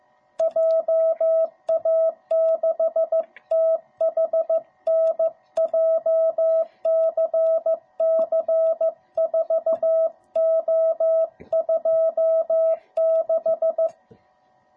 Morse code, heard as a keyed tone, about 15 wpm, 660 Hz. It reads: JA6THN JCC4O26